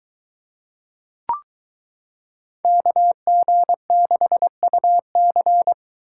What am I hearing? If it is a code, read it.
Morse code, heard as a keyed tone, about 22 wpm, 700 Hz. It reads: KG6UC